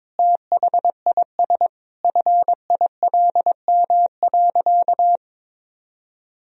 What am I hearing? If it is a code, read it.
Morse code, heard as a keyed tone, about 21 wpm, 700 Hz. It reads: THIS FILM.
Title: Sentences from Top 1000 Words - Review Part 3 - 22wpm